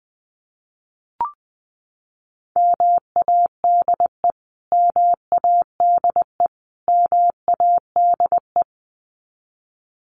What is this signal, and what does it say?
Morse code, heard as a keyed tone, about 20 wpm, 700 Hz. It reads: MADE MADE MADE
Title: Top 200 Words Repeated 20wpm